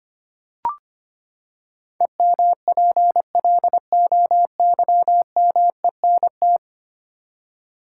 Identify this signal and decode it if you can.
Morse code, heard as a keyed tone, about 25 wpm, 700 Hz. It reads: EMPLOYMENT
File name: Ten Letter Words - 25wpm